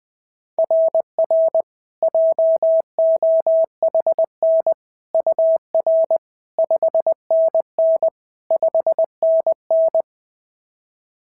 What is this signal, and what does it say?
Morse code, heard as a keyed tone, about 21 wpm, 690 Hz. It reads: RR JOHN UR 5NN 5NN
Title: SOTA QSO Practice 20wpm